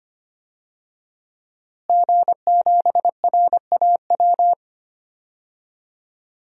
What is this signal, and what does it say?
Morse code, heard as a keyed tone, about 25 wpm, 700 Hz. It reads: G7RAW
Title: DXCC-100 Call Signs - 5 Characters - Repeated 3x - 25wpm